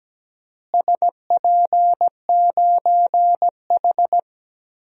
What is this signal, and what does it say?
Morse code, heard as a keyed tone, about 17 wpm, 700 Hz. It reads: SP9H
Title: DXCC-100 Call Signs - 4 Characters - Repeated 3x - 17wpm